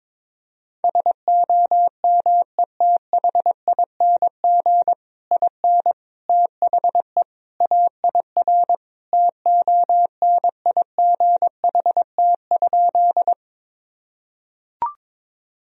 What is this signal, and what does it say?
Morse code, heard as a keyed tone, about 22 wpm, 700 Hz. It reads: SOMETHING IN THE AIR TONIGHT?
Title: Sentences from Top 1000 Words - Review Part 1 - 22wpm